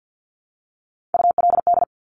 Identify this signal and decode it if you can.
Morse code, heard as a keyed tone, about 50 wpm, 700 Hz. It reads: ULD
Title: Top Three Letter Combinations in Words 50wpm